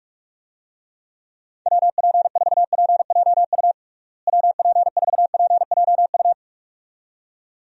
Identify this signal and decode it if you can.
Morse code, heard as a keyed tone, about 46 wpm, 700 Hz. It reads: WP4PJU WP4PJU
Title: USA Call Signs - 6 Characters - Repeated 2x - 45wpm